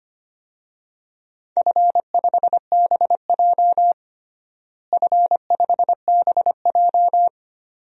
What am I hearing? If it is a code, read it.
Morse code, heard as a keyed tone, about 25 wpm, 700 Hz. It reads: F5BJ F5BJ
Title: DXCC-100 Call Signs - 4 Characters - Repeated 2x - 25wpm